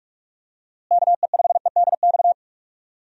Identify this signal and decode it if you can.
Morse code, heard as a keyed tone, about 46 wpm, 700 Hz. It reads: KE5EDX
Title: USA Call Signs - 6 Characters - Repeated 2x - 45wpm